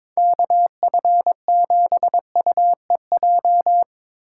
Morse code, heard as a keyed tone, about 22 wpm, 700 Hz. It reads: KF7UEJ
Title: USA Call Signs - 6 Characters - 6x Character Spacing - 22wpm